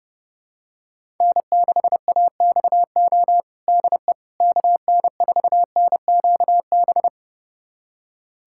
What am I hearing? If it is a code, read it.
Morse code, heard as a keyed tone, about 32 wpm, 700 Hz. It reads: N6AXO DE KN4NQB
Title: USA Callsign DE Callsign 30wpm